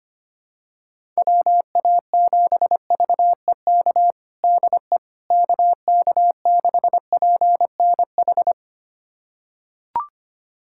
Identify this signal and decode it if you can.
Morse code, heard as a keyed tone, about 25 wpm, 700 Hz. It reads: WA7VEK DE KK6PNH
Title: USA Callsign DE Callsign - 6 Characters - 25wpm